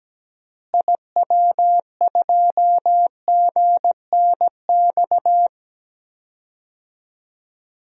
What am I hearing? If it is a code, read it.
Morse code, heard as a keyed tone, about 17 wpm, 700 Hz. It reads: IW2GNX
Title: DXCC-100 Call Signs - 6 Characters - Repeated 3x - 17wpm